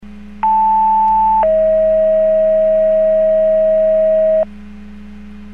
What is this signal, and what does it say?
Two-tone page, heard as a one-second tone, then a three-second tone, about 880 then 640 Hz, with noise. It is heard from 0.4 to 4.4 s.